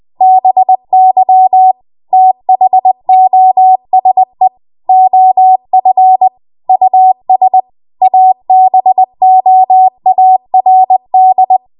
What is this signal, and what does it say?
Morse code, heard as a keyed tone, about 20 wpm, 750 Hz. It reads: BY THOSE OF US ABOARD
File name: Morse / 20WPM